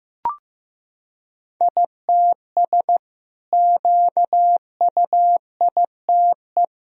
Morse code, heard as a keyed tone, about 15 wpm, 700 Hz. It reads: ITS QUITE
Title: Sentences from Top 1000 Words - Review Part 3 - 15wpm